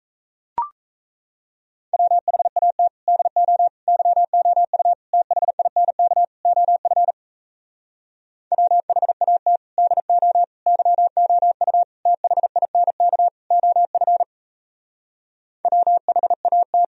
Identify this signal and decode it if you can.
Morse code, heard as a keyed tone, about 37 wpm, 700 Hz. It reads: WHAT DO YOU THINK OF WHAT DO YOU THINK OF WHAT